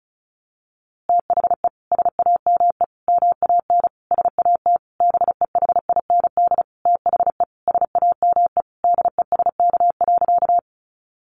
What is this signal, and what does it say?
Morse code, heard as a keyed tone, about 34 wpm, 700 Hz. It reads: THE SAME MAN SAT BEHIND THE SAME DESK.
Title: A Sound Of Thunder 35wpm